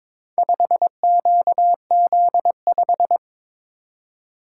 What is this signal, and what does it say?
Morse code, heard as a keyed tone, about 22 wpm, 700 Hz. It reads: 5QZ5
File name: practice / 22wpm / Four Letter-Number Combinations - 22wpm